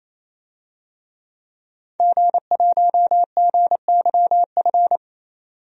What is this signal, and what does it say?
Morse code, heard as a keyed tone, about 27 wpm, 700 Hz. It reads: G1GYF
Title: DXCC-100 Call Signs - 5 Characters - 28wpm